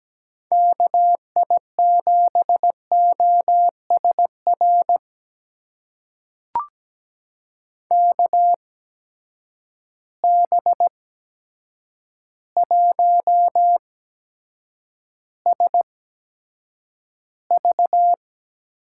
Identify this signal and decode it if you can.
Morse code, heard as a keyed tone, about 17 wpm, spaced as for 8 wpm, 700 Hz. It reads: KI7OSR KB1SV
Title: USA Call Signs - 6 Characters - 8x Character Spacing - 17wpm